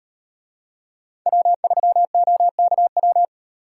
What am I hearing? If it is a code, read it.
Morse code, heard as a keyed tone, about 38 wpm, 700 Hz. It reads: W3OKW